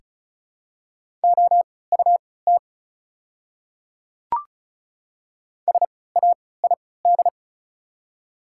Morse code, heard as a keyed tone, about 34 wpm, spaced as for 20 wpm, 700 Hz. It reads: OUT SAID
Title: 3x Character Spacing - Top 100 Words - 35wpm